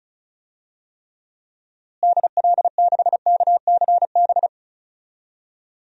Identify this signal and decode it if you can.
Morse code, heard as a keyed tone, about 36 wpm, 700 Hz. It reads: DL6KCB
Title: DXCC-100 Call Signs - 6 Characters - Repeated 3x - 35wpm